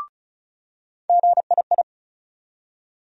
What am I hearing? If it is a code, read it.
Morse code, heard as a keyed tone, about 34 wpm, 700 Hz. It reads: GII
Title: USA Call Sign Postfixes - 3 Letters 35wpm